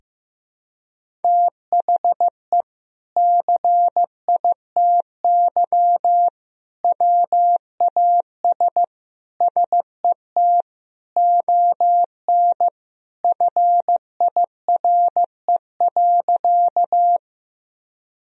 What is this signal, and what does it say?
Morse code, heard as a keyed tone, about 15 wpm, 700 Hz. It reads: THE CITY WAS SET ON FIRE.
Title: Sentences from Top 1000 Words - Review Part 1 - 15wpm